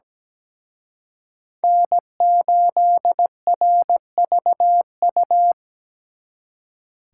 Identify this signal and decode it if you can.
Morse code, heard as a keyed tone, about 20 wpm, 700 Hz. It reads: N8RVU
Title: Speed Racing - USA 1x3 Call Signs 17wpm